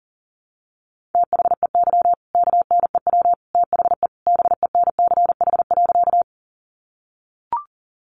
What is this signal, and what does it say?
Morse code, heard as a keyed tone, about 40 wpm, 700 Hz. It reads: THEY KNEW THE BENCH.
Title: Sentences from Top 1400 Words 40wpm